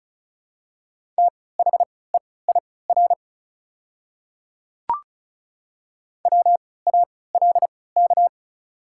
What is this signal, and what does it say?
Morse code, heard as a keyed tone, about 36 wpm, spaced as for 20 wpm, 700 Hz. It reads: THEIR WALK
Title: 3x Character Spacing - Top 300 Words - 35wpm